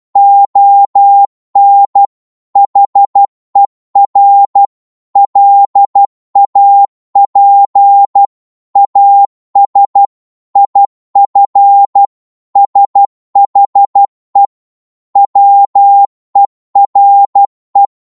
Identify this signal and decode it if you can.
Morse code, heard as a keyed tone, about 12 wpm, 800 Hz. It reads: ON HER LAP AS IF SHE WERE